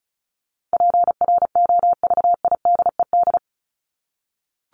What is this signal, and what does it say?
Morse code, heard as a keyed tone, about 34 wpm, 700 Hz. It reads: PROVIDED
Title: Top 1751-2000 Words 35wpm